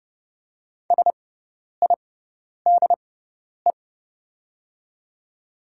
Morse code, heard as a keyed tone, about 31 wpm, spaced as for 10 wpm, 700 Hz. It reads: SIDE
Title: 6x Character Spacing - Top 100 Words - 30wpm